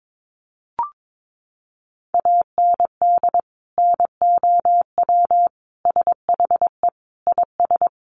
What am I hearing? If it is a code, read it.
Morse code, heard as a keyed tone, about 22 wpm, 700 Hz. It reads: AND NOW SHE IS
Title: Sentences from Top 1500 Words 22wpm